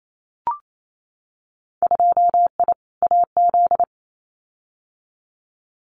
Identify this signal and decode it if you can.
Morse code, heard as a keyed tone, about 28 wpm, 700 Hz. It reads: 2I AZ
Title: Winter Field Day Exchanges 28wpm